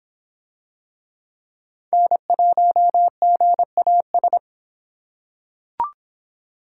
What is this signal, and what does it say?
Morse code, heard as a keyed tone, about 25 wpm, 700 Hz. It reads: N1GAS